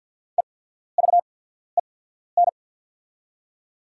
Morse code, heard as a keyed tone, about 49 wpm, spaced as for 14 wpm, 690 Hz. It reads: EVEN
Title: 8x Character Spacing - Top 200 Words - 50wpm